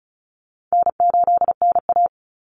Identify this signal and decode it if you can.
Morse code, heard as a keyed tone, about 34 wpm, 700 Hz. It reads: N8NA
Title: Easy Call Signs Repeated 2x - 35wpm